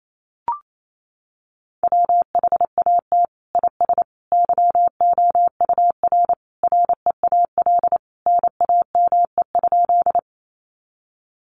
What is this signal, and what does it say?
Morse code, heard as a keyed tone, about 28 wpm, 700 Hz. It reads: WHAT IS YOUR REAL NAME?
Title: Sentences from Top 300 Words 28wpm